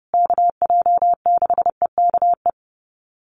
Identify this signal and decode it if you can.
Morse code, heard as a keyed tone, about 30 wpm, 700 Hz. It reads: KJ6EKE
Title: USA Call Signs Repeated 2x 30wpm